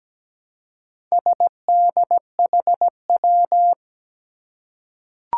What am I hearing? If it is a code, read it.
Morse code, heard as a keyed tone, about 17 wpm, 700 Hz. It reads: SDHW
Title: Four Challenging Letter-Number Combinations 17wpm